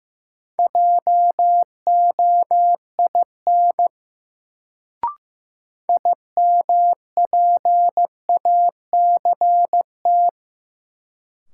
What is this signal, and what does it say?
Morse code, heard as a keyed tone, about 15 wpm, 700 Hz. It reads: JOIN IMPACT
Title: Sentences from Top 1000 Words - Review Part 6 - New Word Drill - 15wpm